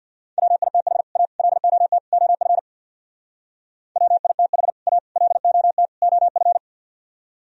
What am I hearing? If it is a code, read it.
Morse code, heard as a keyed tone, about 64 wpm, 700 Hz. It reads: WITH A LOT OF WITH A LOT OF